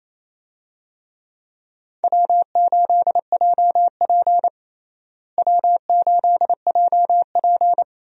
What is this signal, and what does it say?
Morse code, heard as a keyed tone, about 28 wpm, 700 Hz. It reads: W8JP W8JP
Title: USA Call Signs - 4 Characters - Repeated 3x - 28wpm